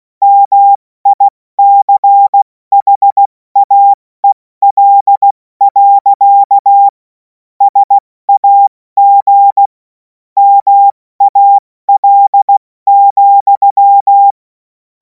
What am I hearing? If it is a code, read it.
Morse code, heard as a keyed tone, about 16 wpm, spaced as for 14 wpm, 800 Hz. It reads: MICHAEL. SAG MAL,
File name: Morsekurs / Witze / Witz 07. April 2026 70 F80